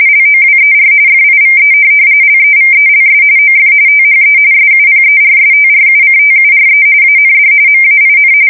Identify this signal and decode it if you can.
RTTY, about 45.45 baud. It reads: LOWER FIVE LEVELS OF THE SUPERSTRUCTURE AND MAIN
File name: fsk170 / USS-New-Jersey